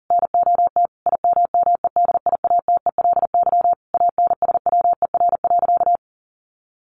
Morse code, heard as a keyed tone, about 37 wpm, 700 Hz. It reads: NOT IMMEDIATELY ANSWER.
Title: Sentences from Top 1100 Words 40wpm